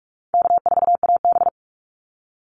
Morse code, heard as a keyed tone, about 44 wpm, 700 Hz. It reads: K4AB
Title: Easy Call Signs Repeated 3x - 45wpm